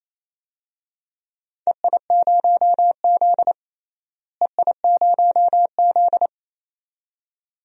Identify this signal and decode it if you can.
Morse code, heard as a keyed tone, about 28 wpm, 700 Hz. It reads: EI0Z EI0Z
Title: DXCC-100 Call Signs - 4 Characters - Repeated 2x - 28wpm